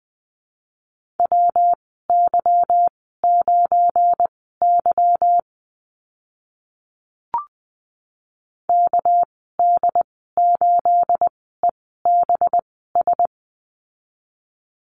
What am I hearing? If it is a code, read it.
Morse code, heard as a keyed tone, about 20 wpm, spaced as for 15 wpm, 700 Hz. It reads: WY9Y KD8EBS
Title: 219 Lesson 35g - Number 0 - Call Signs 20wpm